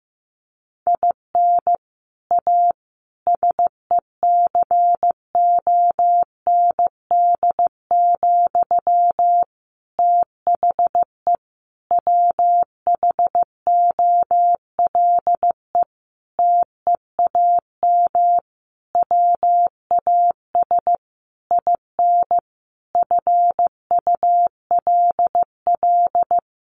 Morse code, heard as a keyed tone, about 15 wpm, 700 Hz. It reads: IN A SECOND, THE WHOLE TEAM WAS IN FULL
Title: Sentences from Top 1500 Words 15wpm